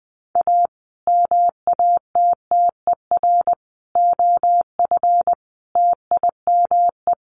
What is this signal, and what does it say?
Morse code, heard as a keyed tone, about 20 wpm, 700 Hz. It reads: A MATTER OF TIME
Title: Sets of 5 Words Encore 20wpm